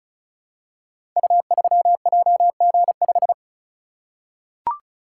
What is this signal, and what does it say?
Morse code, heard as a keyed tone, about 34 wpm, 700 Hz. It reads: U3JG5
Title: Code Groups - 5 Letter:Numbers - 35wpm